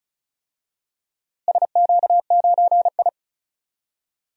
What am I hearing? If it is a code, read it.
Morse code, heard as a keyed tone, about 34 wpm, 700 Hz. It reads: SQ9I
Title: DXCC-100 Call Signs - 4 Characters - Repeated 2x - 35wpm